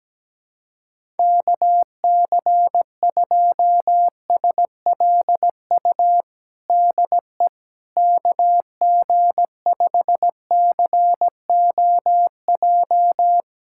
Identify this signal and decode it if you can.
Morse code, heard as a keyed tone, about 17 wpm, 700 Hz. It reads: KC2SLU DE KG5COJ
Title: USA Callsign DE Callsign - 6 Characters - 17wpm